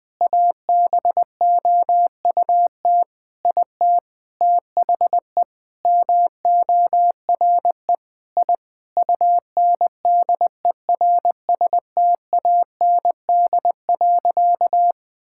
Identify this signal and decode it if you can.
Morse code, heard as a keyed tone, about 20 wpm, 700 Hz. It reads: ABOUT IT THE MORE I UNDERSTAND.
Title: Sentences with 9-Word Phrases - 20wpm